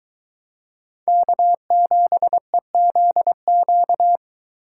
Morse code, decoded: K7EZQ

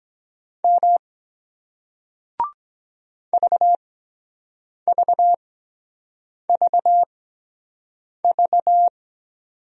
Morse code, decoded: M VVVV

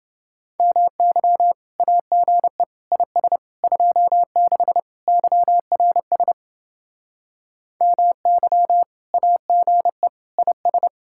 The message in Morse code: MY AGE IS 26 YRS MY AGE IS

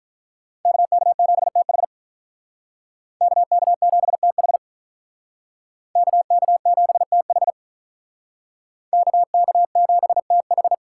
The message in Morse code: KK7TH KK7TH KK7TH KK7TH

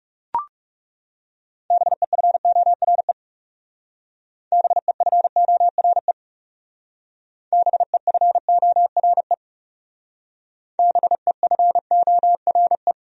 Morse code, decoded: BEFORE BEFORE BEFORE BEFORE

